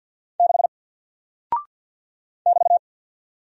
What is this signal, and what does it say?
Morse code, heard as a keyed tone, about 48 wpm, 700 Hz. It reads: 6 <BT>
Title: Single Character (Letter:Number:Punctuation:Prosign) Alt 50wpm